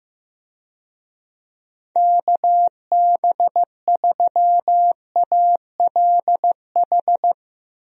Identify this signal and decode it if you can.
Morse code, heard as a keyed tone, about 15 wpm, 700 Hz. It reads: KB3ALH
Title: USA Call Signs - 6 Characters - Repeated 3x - 15wpm